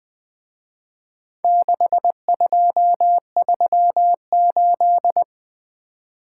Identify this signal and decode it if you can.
Morse code, heard as a keyed tone, about 23 wpm, 700 Hz. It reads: 6238
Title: Speed-Racing - Four Letter-Number Combinations - 20wpm